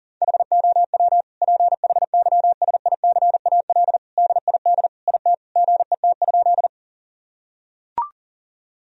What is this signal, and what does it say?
Morse code, heard as a keyed tone, about 40 wpm, 700 Hz. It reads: HOW PHYSICAL DID IT GET?
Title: Sentences from Top 1000 Words - Review Part 4 - 40wpm